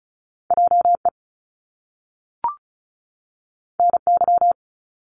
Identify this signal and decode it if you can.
Morse code, heard as a keyed tone, about 36 wpm, 700 Hz. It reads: JE NY
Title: Top Two Letter Combinations in Words 35wpm